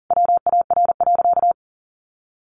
Morse code, heard as a keyed tone, about 40 wpm, 700 Hz. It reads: WAR.
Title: Sentences from Top 1300 Words 40wpm